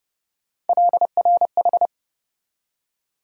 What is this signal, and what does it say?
Morse code, heard as a keyed tone, about 31 wpm, 700 Hz. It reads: LRH
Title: USA Call Sign Postfixes - 3 Letters 30wpm